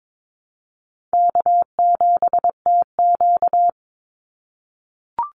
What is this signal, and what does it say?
Morse code, heard as a keyed tone, about 22 wpm, 700 Hz. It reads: K7TQ
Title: NA SOTA Chaser Callsigns 22wpm